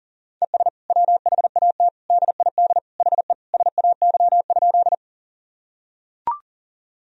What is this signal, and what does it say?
Morse code, heard as a keyed tone, about 40 wpm, 700 Hz. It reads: ES WHAT DID HE SAY?
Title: Sentences from Top 200 Words and Abbreviations 40wpm